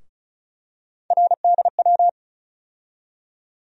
Morse code, heard as a keyed tone, about 36 wpm, 700 Hz. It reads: RDW
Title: USA Call Sign Postfixes - 3 Letters 35wpm